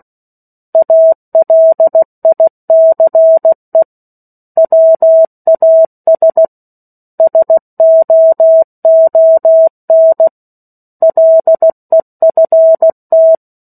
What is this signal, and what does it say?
Morse code, heard as a keyed tone, about 16 wpm, 650 Hz. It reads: ALICE WAS SOON LEFT
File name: 03-Chapter-03 16 WPM